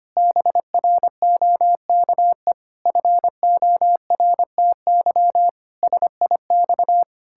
Morse code, heard as a keyed tone, about 25 wpm, 700 Hz. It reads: BROKE FORTY SIX
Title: Pangram Challenge v2 - 25wpm